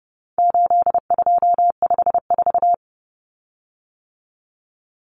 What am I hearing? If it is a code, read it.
Morse code, heard as a keyed tone, about 30 wpm, 700 Hz. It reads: 8254
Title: Common Cut Numbers - 4-Digit 30wpm